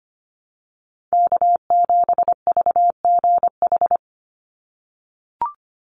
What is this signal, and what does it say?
Morse code, heard as a keyed tone, about 25 wpm, 690 Hz. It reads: K7VGH
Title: USA Call Signs 25wpm